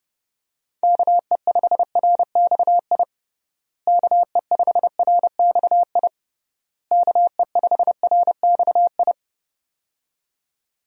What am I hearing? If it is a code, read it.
Morse code, heard as a keyed tone, about 30 wpm, 700 Hz. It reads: KE5RXI KE5RXI KE5RXI
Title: USA Call Signs - 6 Characters - Repeated 3x - 30wpm